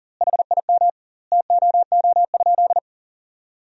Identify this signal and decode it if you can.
Morse code, heard as a keyed tone, about 40 wpm, 700 Hz. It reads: HIM TOO?
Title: 2x Word-Spacing - Sentences from Top 500 Words - 40wpm